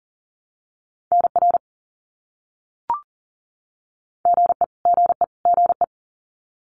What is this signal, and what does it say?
Morse code, heard as a keyed tone, about 40 wpm, 700 Hz. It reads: NR GE GE GE
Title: Abbreviations Repeated 3x - 40wpm